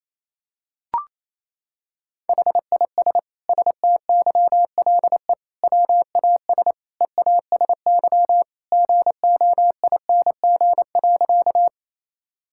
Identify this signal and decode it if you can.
Morse code, heard as a keyed tone, about 28 wpm, 700 Hz. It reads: HIS STYLE WAS EASY GOING.
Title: Sentences from Top 1000 Words - Review Part 5 - 28wpm